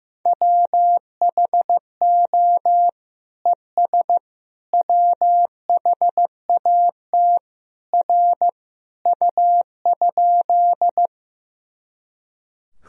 Morse code, heard as a keyed tone, about 15 wpm, 700 Hz. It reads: WHO ES WHAT R U?
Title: Sentences from Top 100 Words and Abbreviations 15wpm